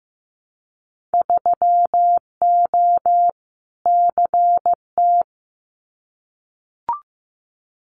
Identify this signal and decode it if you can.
Morse code, heard as a keyed tone, about 15 wpm, 700 Hz. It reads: 3O CT
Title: Winter Field Day Exchanges 15wpm